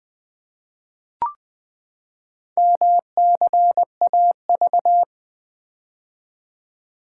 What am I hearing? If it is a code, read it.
Morse code, heard as a keyed tone, about 20 wpm, 700 Hz. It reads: MCAV
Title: Four Letter-Number Combinations - 20wpm